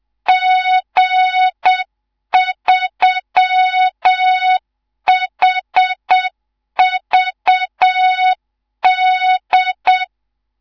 Morse code, heard as a keyed tone, about 7 wpm, 3700 Hz, with noise. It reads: G3HVD
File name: Morze-HA2MN / hang / 19 GYAKORLAT